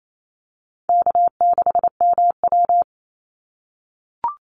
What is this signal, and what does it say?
Morse code, heard as a keyed tone, about 28 wpm, 700 Hz. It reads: K6MW
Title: NA SOTA Chaser Callsigns 28wpm